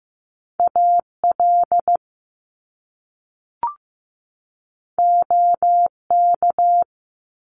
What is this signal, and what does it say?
Morse code, heard as a keyed tone, about 15 wpm, 700 Hz. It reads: AL OK
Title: Top Two Letter Combinations in Words 15wpm